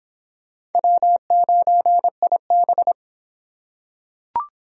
Morse code, heard as a keyed tone, about 25 wpm, 700 Hz. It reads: W9IB